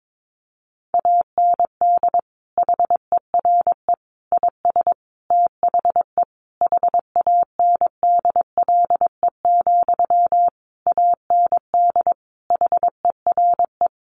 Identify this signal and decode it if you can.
Morse code, heard as a keyed tone, about 22 wpm, 700 Hz. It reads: AND HERE IS THE HANDLE, AND HERE
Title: Sentences from Top 1100 Words 22wpm